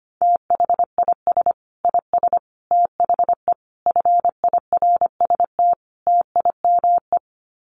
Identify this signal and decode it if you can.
Morse code, heard as a keyed tone, about 25 wpm, 700 Hz. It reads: THIS IS THE FIRST TIME
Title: Sets of 5 Words 25wpm